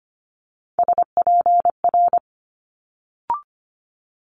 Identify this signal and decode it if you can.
Morse code, heard as a keyed tone, about 25 wpm, 700 Hz. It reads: SPR
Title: Multi-letter Phonemes Quick Fire 25wpm